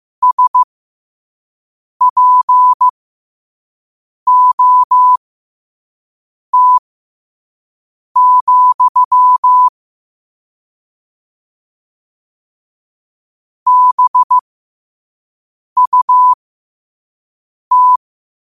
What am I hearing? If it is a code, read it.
Morse code, heard as a keyed tone, about 15 wpm, spaced as for 5 wpm, 1000 Hz. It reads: SPOT, BUT
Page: New quotes every day in morse code at 5 Words per minute.